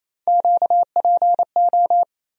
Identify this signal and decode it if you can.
Morse code, heard as a keyed tone, about 28 wpm, 700 Hz. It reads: QPO
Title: USA Call Sign Postfixes - 3 Letters 28wpm